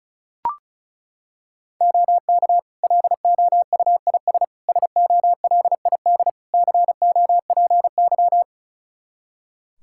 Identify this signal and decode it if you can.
Morse code, heard as a keyed tone, about 36 wpm, 700 Hz. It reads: OK LOUIS SOLID COPY
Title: Practice QSO - Exchange 3 - More Info - 35wpm